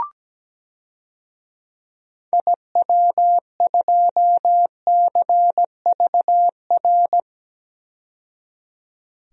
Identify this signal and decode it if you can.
Morse code, heard as a keyed tone, about 17 wpm, 700 Hz. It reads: IW2CVR
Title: DXCC-100 Call Signs - 6 Characters - 17wpm